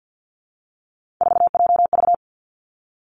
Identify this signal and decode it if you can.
Morse code, heard as a keyed tone, about 50 wpm, 700 Hz. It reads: 4JV
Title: Three Challenging Letter-Number Combinations 50wpm